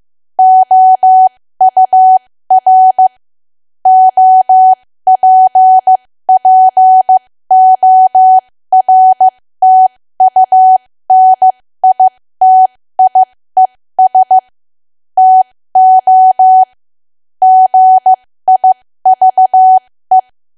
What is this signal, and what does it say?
Morse code, heard as a keyed tone, about 15 wpm, spaced as for 13 wpm, 750 Hz. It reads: OUR OPPORTUNITIES TO GIVE
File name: cwa / W1AW-Level2 / 13WPM